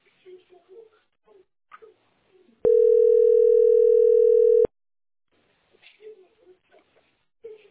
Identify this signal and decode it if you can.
Ringback tone (North America), 440 + 480 Hz.